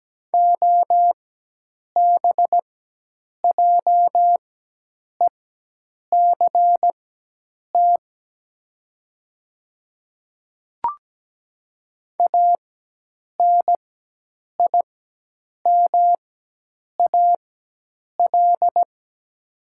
Morse code, heard as a keyed tone, about 17 wpm, spaced as for 8 wpm, 700 Hz. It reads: OBJECT ANIMAL
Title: 4x Character Spacing - Top 500 Words - 17wpm